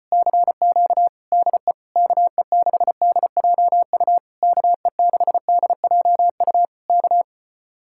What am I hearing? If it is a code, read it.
Morse code, heard as a keyed tone, about 33 wpm, 700 Hz. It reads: CQ DE KE6DJU KE6DJU K